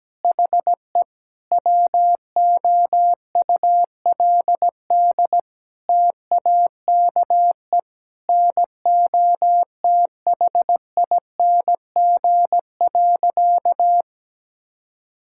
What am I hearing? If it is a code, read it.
Morse code, heard as a keyed tone, about 17 wpm, 700 Hz. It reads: HE WOULD TAKE NOTHING.
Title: Sentences from Top 1000 Words - Review Part 6 - 17wpm